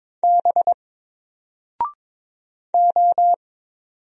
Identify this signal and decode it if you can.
Morse code, heard as a keyed tone, about 22 wpm, 700 Hz. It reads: B O
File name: Single-Letter-Number-22wpm